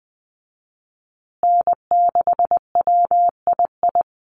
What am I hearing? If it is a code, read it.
Morse code, heard as a keyed tone, about 20 wpm, 700 Hz. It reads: N6WII
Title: USA Call Signs Repeated 3x 20wpm